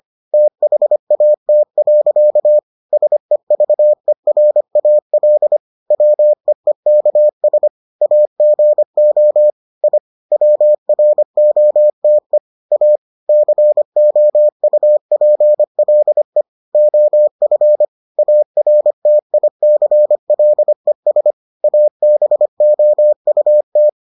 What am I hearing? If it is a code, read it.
Morse code, heard as a keyed tone, about 25 wpm, 600 Hz. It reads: THAT. SEVERAL WEEKS AGO I WROTE A COUPLE OF ARTICLES ABOUT